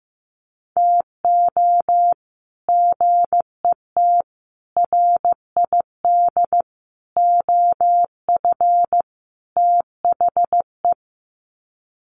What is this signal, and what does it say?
Morse code, heard as a keyed tone, about 15 wpm, 700 Hz. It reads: TO GET RID OF THE
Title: Sets of 5 Words 15wpm